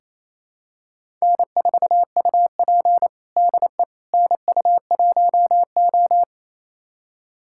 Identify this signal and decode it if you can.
Morse code, heard as a keyed tone, about 28 wpm, 700 Hz. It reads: N4UP DE NU1O
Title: USA Callsign DE Callsign - 4 Characters - 28wpm